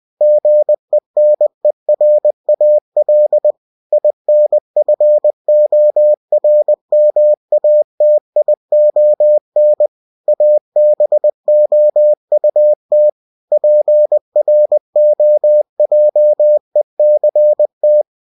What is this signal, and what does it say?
Morse code, heard as a keyed tone, about 20 wpm, 600 Hz. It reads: GENERAL INFORMATION ABOUT PROJECT
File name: cw-books / winnie-the-pooh / winnie-20wpm-page-0122